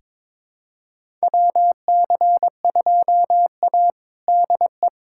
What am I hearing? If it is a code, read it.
Morse code, heard as a keyed tone, about 22 wpm, 700 Hz. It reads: WC2A DE
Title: USA Callsign DE Callsign - 4 Characters - 22wpm